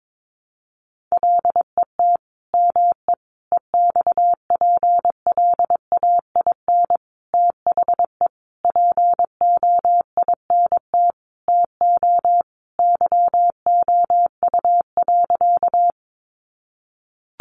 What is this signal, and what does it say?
Morse code, heard as a keyed tone, about 22 wpm, 700 Hz. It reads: LET ME EXPLAIN THE POINT TO YOU.
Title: Sentences from Top 1100 Words 22wpm